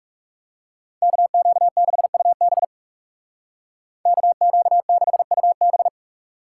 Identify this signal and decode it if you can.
Morse code, heard as a keyed tone, about 36 wpm, 700 Hz. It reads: KQ6UB KQ6UB
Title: Speed Racing - USA 2x2 Call Signs 30wpm